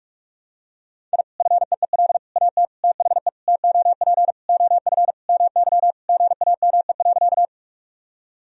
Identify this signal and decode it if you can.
Morse code, heard as a keyed tone, about 46 wpm, 700 Hz. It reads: I FEEL AT THE TOP OF MY GAME.
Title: Sentences from Top 1000 Words - Review Part 2 - 45wpm